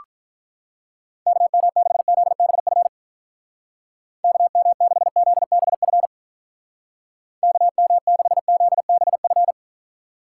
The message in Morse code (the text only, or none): KM6ZBF KM6ZBF KM6ZBF